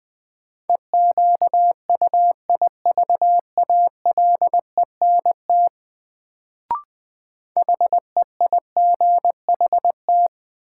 Morse code, heard as a keyed tone, about 20 wpm, 700 Hz. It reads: EQUIVALENT HEIGHT
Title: Sentences from Top 1600 Words - New Word Drill - 20wpm